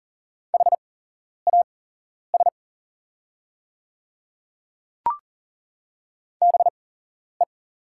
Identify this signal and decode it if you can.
Morse code, heard as a keyed tone, about 40 wpm, spaced as for 11 wpm, 700 Hz. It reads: HAS BE
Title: 8x Character Spacing - Top 100 Words - 40wpm